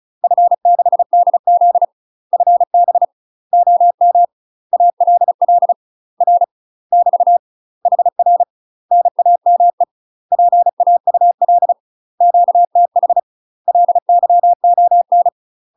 Morse code, heard as a keyed tone, about 36 wpm, 700 Hz. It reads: F6DZ FB OM ALL R <BT> HR NAME PAUL QTH LYON
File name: UK MASTER QSOs TRAINING / 35WPM / UK MASTER QSOs 0023